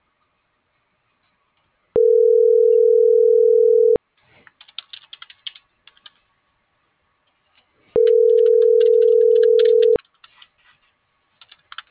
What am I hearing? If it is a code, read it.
Ringback tone (North America), 440 + 480 Hz.